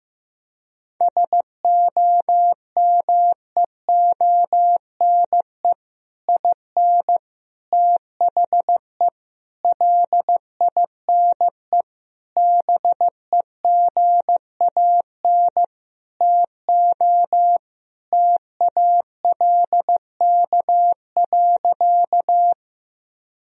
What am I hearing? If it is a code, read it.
Morse code, heard as a keyed tone, about 15 wpm, 700 Hz. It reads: SOMEONE IN THE LINE BEGAN TO TALK.
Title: Sentences from Top 1000 Words - Review Part 1 - 15wpm